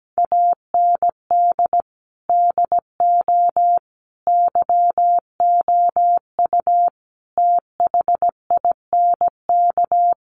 Morse code, heard as a keyed tone, about 17 wpm, 700 Hz. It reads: AND DO YOU THINK
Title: Sentences from Top 600 Words 17wpm